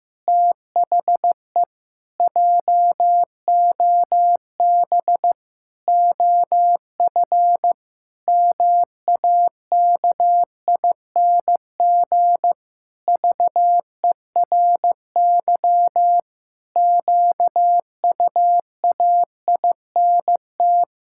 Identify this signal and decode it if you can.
Morse code, heard as a keyed tone, about 16 wpm, 700 Hz. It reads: THE JOB OF MAKING VERY QUAINT
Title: Pangram Challenge v2 - 15wpm